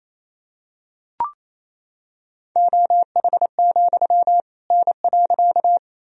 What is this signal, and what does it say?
Morse code, heard as a keyed tone, about 28 wpm, 700 Hz. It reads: OH, N.
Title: Sentences from Top 500 Words and Abbreviations 28wpm